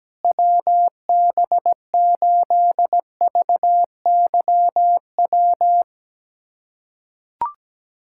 Morse code, heard as a keyed tone, about 17 wpm, 700 Hz. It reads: WB8VYW
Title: USA Callsign DE Callsign 17wpm